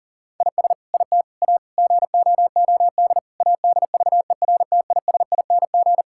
Morse code, heard as a keyed tone, about 40 wpm, 700 Hz. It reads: IS IT A GOOD ADVERTISING
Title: Sentences from Top 1000 Words - Review Part 6 - 40wpm